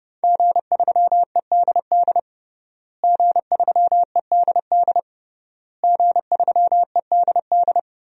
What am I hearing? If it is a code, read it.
Morse code, heard as a keyed tone, about 32 wpm, 700 Hz. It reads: G3EDD G3EDD G3EDD
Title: DXCC-100 Call Signs - 5 Characters - Repeated 3x - 30wpm